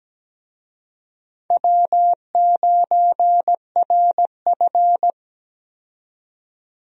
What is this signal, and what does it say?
Morse code, heard as a keyed tone, about 17 wpm, 700 Hz. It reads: W9RF
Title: USA Call Signs - 4 Characters - 17wpm